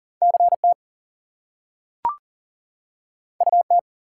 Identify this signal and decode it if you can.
Morse code, heard as a keyed tone, about 40 wpm, 700 Hz. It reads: CT UT
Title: US States and Canadian Provinces 40wpm